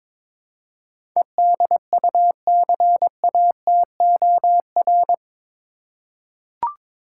Morse code, heard as a keyed tone, about 22 wpm, 700 Hz. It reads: EDUCATOR
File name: practice / 22wpm / Eight Letter Words - 22wpm